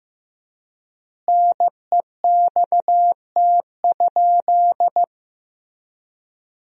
Morse code, heard as a keyed tone, about 16 wpm, 700 Hz. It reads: NEXT?
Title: 3x Word-Spacing - Sentences from Top 300 Words - 15wpm